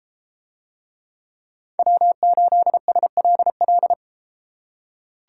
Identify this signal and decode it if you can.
Morse code, heard as a keyed tone, about 33 wpm, 700 Hz. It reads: W8SLL